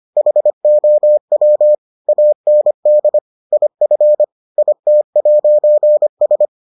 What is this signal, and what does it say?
Morse code, heard as a keyed tone, about 25 wpm, 600 Hz. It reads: HOW AND IF IT'S